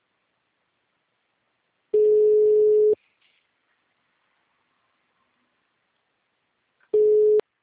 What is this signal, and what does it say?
Ringback tone (Europe, 425 Hz).
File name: out- -2025- - - .44714